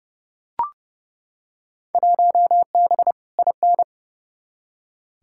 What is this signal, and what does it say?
Morse code, heard as a keyed tone, about 30 wpm, 700 Hz. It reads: 1B IN
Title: Field Day Exchanges Encore - 30wpm